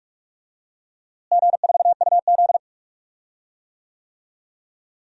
Morse code, heard as a keyed tone, about 46 wpm, 700 Hz. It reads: G4UZ
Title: DXCC-100 Call Signs - 4 Characters - Repeated 2x - 45wpm